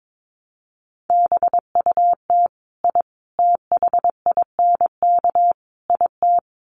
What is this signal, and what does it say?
Morse code, heard as a keyed tone, about 22 wpm, 700 Hz. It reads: BUT I THINK IT
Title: Sets of 4 Words 22wpm